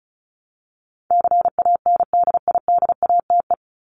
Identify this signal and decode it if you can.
Morse code, heard as a keyed tone, about 34 wpm, 700 Hz. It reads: CANDIDATE
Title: Top 1501-1750 Words 35wpm